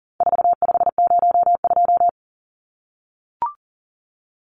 Morse code, heard as a keyed tone, about 40 wpm, 700 Hz. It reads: 4502
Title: Numbers - 4-Digit 40wpm